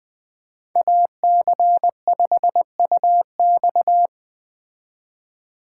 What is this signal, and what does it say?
Morse code, heard as a keyed tone, about 20 wpm, 700 Hz. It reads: AC5UX